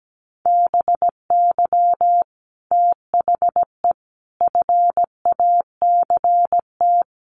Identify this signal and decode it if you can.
Morse code, heard as a keyed tone, about 17 wpm, 700 Hz. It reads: BY THE FACT
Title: Sets of 4 Words 17wpm